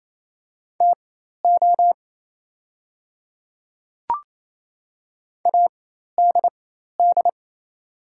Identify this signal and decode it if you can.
Morse code, heard as a keyed tone, about 28 wpm, spaced as for 13 wpm, 700 Hz. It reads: TO ADD
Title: 4x Character Spacing - Top 200 Words - 28wpm